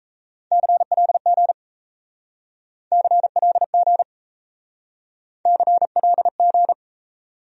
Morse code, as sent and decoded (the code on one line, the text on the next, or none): -.-. .-.. --. / -.-. .-.. --. / -.-. .-.. --.
CLG CLG CLG